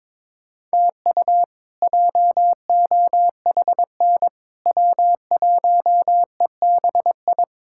Morse code, heard as a keyed tone, about 22 wpm, 700 Hz. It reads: TU JOHN W1EBI
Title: K1USN SST QSO Practice - 22wpm